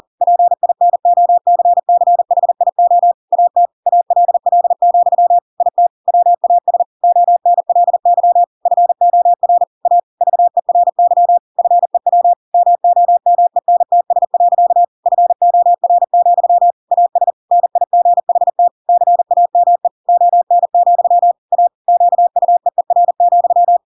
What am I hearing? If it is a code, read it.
Morse code, heard as a keyed tone, about 40 wpm, 700 Hz. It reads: PINOCCHIO AT ALL, IT WAS ONLY FOR A VERY FEW MOMENTS. FOR, AS NIGHT CAME ON, A QUEER,
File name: Pinocchio / Pinocchio ch5 40 40 ik1hge